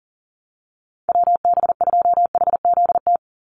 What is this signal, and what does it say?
Morse code, heard as a keyed tone, about 40 wpm, 700 Hz. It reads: WB2HZT
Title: USA Call Signs 40wpm